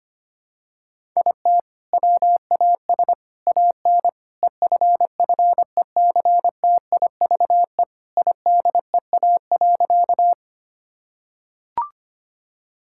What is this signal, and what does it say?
Morse code, heard as a keyed tone, about 25 wpm, 700 Hz. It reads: IT WAS AN EFFECTIVE IDEA.
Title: Sentences from Top 1000 Words - Review Part 6 - 25wpm